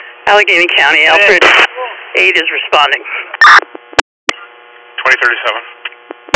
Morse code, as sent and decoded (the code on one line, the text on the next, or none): none